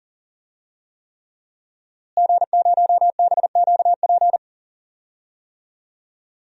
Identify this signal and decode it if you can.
Morse code, heard as a keyed tone, about 40 wpm, 700 Hz. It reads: G0BQP
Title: DXCC-100 Call Signs - 5 Characters - Repeated 3x - 40wpm